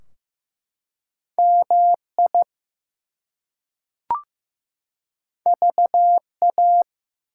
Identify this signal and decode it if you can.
Morse code, heard as a keyed tone, about 15 wpm, 700 Hz. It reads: MI VA
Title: US States and Canadian Provinces 15wpm